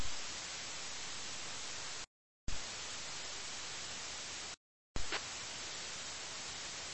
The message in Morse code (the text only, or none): none